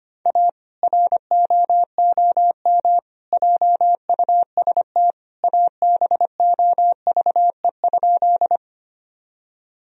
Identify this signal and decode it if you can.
Morse code, heard as a keyed tone, about 25 wpm, 700 Hz. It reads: A ROOM JUST ABOVE?
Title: Sentences from Top 1000 Words - Review Part 2 - 25wpm